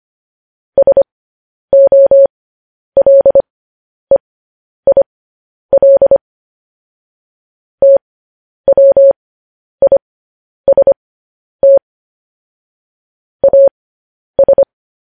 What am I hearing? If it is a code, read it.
Morse code, heard as a keyed tone, about 25 wpm, spaced as for 10 wpm, 550 Hz. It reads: SOLEIL TWIST AS
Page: news-headlines-in-morse-code-at-10-wpm